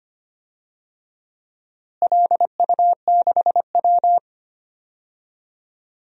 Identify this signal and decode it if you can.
Morse code, heard as a keyed tone, about 25 wpm, 700 Hz. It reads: LU6W
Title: DXCC-100 Call Signs - 4 Characters - Repeated 2x - 25wpm